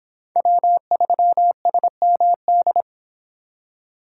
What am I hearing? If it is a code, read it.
Morse code, heard as a keyed tone, about 24 wpm, 700 Hz. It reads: W3SMD